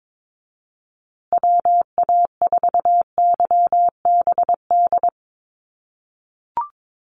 Morse code, heard as a keyed tone, about 22 wpm, 700 Hz. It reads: WA4YBD
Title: USA Call Signs Repeated 3x 22wpm